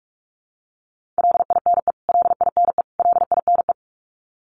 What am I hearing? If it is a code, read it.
Morse code, heard as a keyed tone, about 44 wpm, 700 Hz. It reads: LINE LINE LINE
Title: Top 300 Words Repeated 45wpm